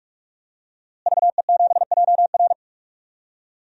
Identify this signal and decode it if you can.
Morse code, heard as a keyed tone, about 45 wpm, 700 Hz. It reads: VE7JR
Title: DXCC-100 Call Signs - 5 Characters - Repeated 2x - 45wpm